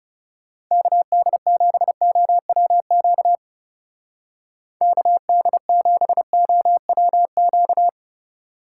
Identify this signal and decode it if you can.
Morse code, heard as a keyed tone, about 36 wpm, 700 Hz. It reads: KD7OWQ KD7OWQ
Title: Speed Racing - USA 2x3 Call Signs 30wpm